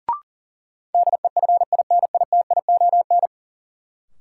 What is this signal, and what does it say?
Morse code, heard as a keyed tone, about 38 wpm, 700 Hz. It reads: DEFINITION
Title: Sentences from Top 1000 Words - Review Part 4 - New Word Drill - 40wpm